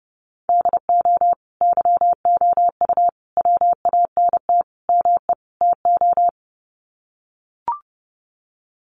Morse code, heard as a keyed tone, about 30 wpm, 700 Hz. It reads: DO YOU WANT ME TO
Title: Sets of 5 Words 30wpm